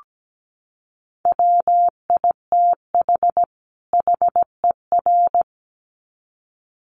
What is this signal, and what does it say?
Morse code, heard as a keyed tone, about 17 wpm, 700 Hz. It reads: WITH HER
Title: Sets of 2 Words 17wpm